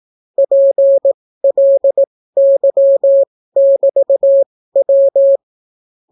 Morse code, heard as a keyed tone, about 18 wpm, spaced as for 14 wpm, 550 Hz. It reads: PLY<BT>W